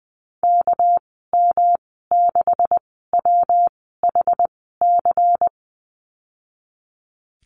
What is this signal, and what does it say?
Morse code, heard as a keyed tone, about 20 wpm, spaced as for 15 wpm, 700 Hz. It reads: KM6WHC
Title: 213 Lesson 34g - Number 8 - Call Signs 20wpm